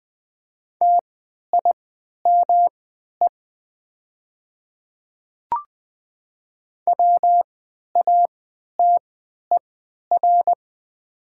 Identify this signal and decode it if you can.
Morse code, heard as a keyed tone, about 20 wpm, spaced as for 11 wpm, 700 Hz. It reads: TIME WATER
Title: 3x Character Spacing - Top 100 Words - 20wpm